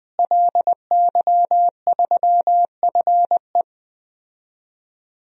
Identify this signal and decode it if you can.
Morse code, heard as a keyed tone, about 20 wpm, 700 Hz. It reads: LY3FE
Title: DXCC-100 Call Signs - 5 Characters - Repeated 3x - 20wpm